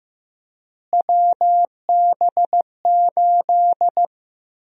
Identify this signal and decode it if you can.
Morse code, heard as a keyed tone, about 15 wpm, 700 Hz. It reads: WB8
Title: USA Call Sign Prefixes - Repeated 2x - 15wpm